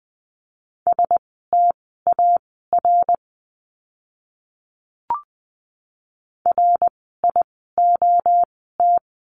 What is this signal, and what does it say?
Morse code, heard as a keyed tone, about 20 wpm, spaced as for 15 wpm, 700 Hz. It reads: STAR RIOT
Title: 061 Lesson 11f - Letter h - Words 20wpm